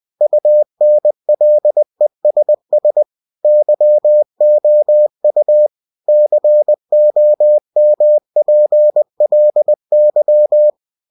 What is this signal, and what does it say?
Morse code, heard as a keyed tone, about 20 wpm, 600 Hz. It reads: UNLESS YOU COMPLY
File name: cw-books / winnie-the-pooh / winnie-20wpm-page-0116